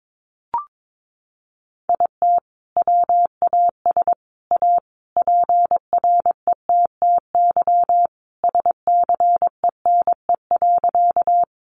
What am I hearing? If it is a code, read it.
Morse code, heard as a keyed tone, about 22 wpm, 700 Hz. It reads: IT WAS A PRETTY SCENE.
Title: Sentences from Top 1100 Words 22wpm